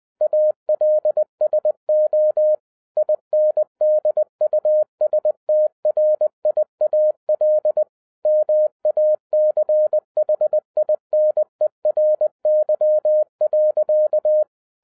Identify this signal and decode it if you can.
Morse code, heard as a keyed tone, about 20 wpm, 610 Hz. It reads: ALSO INDUSTRIAL MACHINERY.